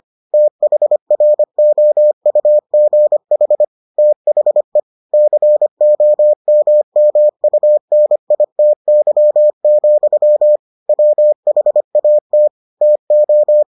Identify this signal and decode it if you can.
Morse code, heard as a keyed tone, about 25 wpm, 600 Hz. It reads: THROUGH THE COMMUNITY, WHAT TO